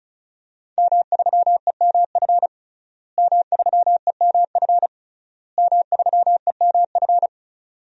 Morse code, heard as a keyed tone, about 36 wpm, 700 Hz. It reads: M3EMF M3EMF M3EMF
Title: DXCC-100 Call Signs - 5 Characters - Repeated 3x - 35wpm